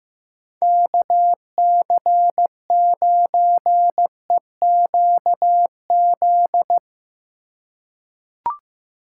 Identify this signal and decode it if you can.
Morse code, heard as a keyed tone, about 15 wpm, 700 Hz. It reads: KC9EQZ
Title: USA Call Signs - 6 Characters - 3x Character Spacing - 15wpm